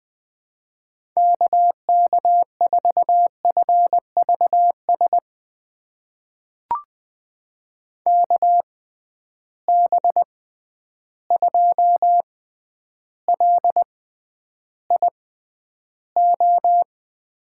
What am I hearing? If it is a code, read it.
Morse code, heard as a keyed tone, about 20 wpm, spaced as for 10 wpm, 700 Hz. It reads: KK4FVS KB2LIO
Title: USA Call Signs - 6 Characters - 6x Character Spacing - 20wpm